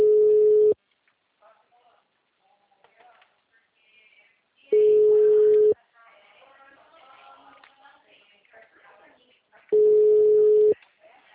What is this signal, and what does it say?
Ringback tone (Europe, 425 Hz), with speech.